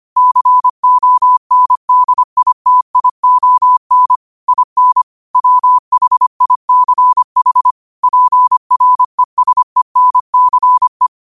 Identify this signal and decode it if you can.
Morse code, heard as a keyed tone, about 25 wpm, 1000 Hz. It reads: CONDITION IN WHICH PRESENCE
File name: qotdfiles / 25wpm